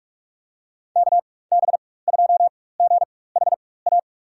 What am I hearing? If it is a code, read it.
Morse code, heard as a keyed tone, about 44 wpm, spaced as for 21 wpm, 690 Hz. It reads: KB2GHA
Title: USA Call Signs - 4x Character Spacing - 45wpm